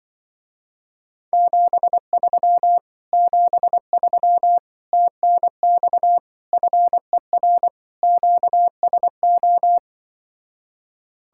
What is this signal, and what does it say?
Morse code, heard as a keyed tone, about 24 wpm, 700 Hz. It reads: 73 73 TNX FER QSO
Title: Speed-Racing Common QSO Phrases Encore - 20wpm